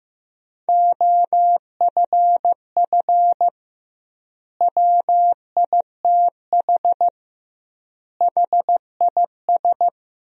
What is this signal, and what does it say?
Morse code, heard as a keyed tone, about 15 wpm, 700 Hz. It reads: OFF WITH HIS
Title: 2x Word-Spacing - Sentences from Top 300 Words - 15wpm